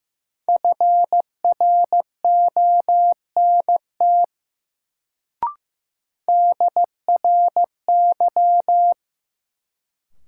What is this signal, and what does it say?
Morse code, heard as a keyed tone, about 15 wpm, 700 Hz. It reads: FRONT DRY
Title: Sentences from Top 500 Words - New Word Drill - 15wpm